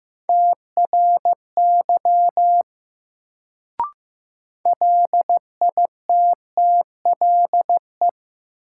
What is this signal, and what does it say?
Morse code, heard as a keyed tone, about 15 wpm, 700 Hz. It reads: TRY LITTLE
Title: Sentences from Top 200 Words - New Word Drill - 15wpm